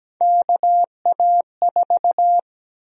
Morse code, heard as a keyed tone, about 17 wpm, 700 Hz. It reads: KA4
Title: USA Call Sign Prefixes - Repeated 2x - 17wpm